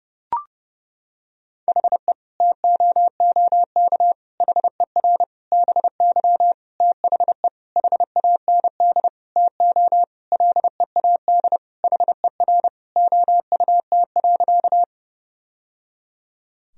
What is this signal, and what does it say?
Morse code, heard as a keyed tone, about 30 wpm, 700 Hz. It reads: HE TOOK HER BY THE HAND TO LEAD HER OUT.
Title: Sentences from Top 1000 Words - Review Part 2 - 30wpm